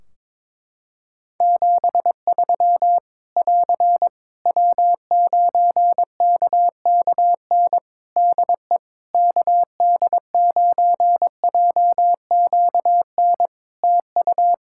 Morse code, heard as a keyed tone, about 22 wpm, 700 Hz. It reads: 73 <AR> W9KKN DE KD9JQN TU
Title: Practice QSO - Exchange 4 - Ending - 22wpm